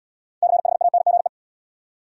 Morse code, heard as a keyed tone, about 67 wpm, 700 Hz. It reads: CHANGE